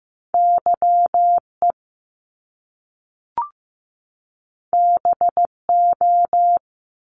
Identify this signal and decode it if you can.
Morse code, heard as a keyed tone, about 15 wpm, 700 Hz. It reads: YE BO
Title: Top Two Letter Combinations in Words 15wpm